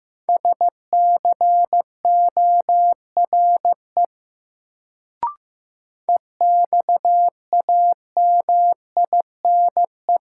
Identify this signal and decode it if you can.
Morse code, heard as a keyed tone, about 15 wpm, 700 Hz. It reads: SCORE EXAMINE
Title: Sentences from Top 1600 Words - New Word Drill - 15wpm